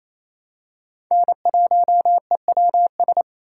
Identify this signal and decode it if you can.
Morse code, heard as a keyed tone, about 28 wpm, 700 Hz. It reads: N1EWS
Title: USA Call Signs - 5 Characters - 4x Character Spacing - 28wpm